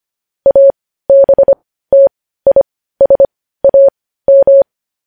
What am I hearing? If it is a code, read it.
Morse code, heard as a keyed tone, about 25 wpm, spaced as for 15 wpm, 550 Hz. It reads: ABTISAM
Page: news-headlines-in-morse-code-at-15-wpm